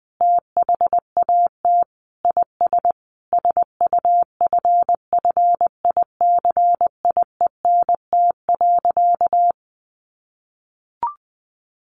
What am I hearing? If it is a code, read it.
Morse code, heard as a keyed tone, about 20 wpm, 700 Hz. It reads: THAT IS SUFFICIENT.
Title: Sentences from Top 1400 Words 20wpm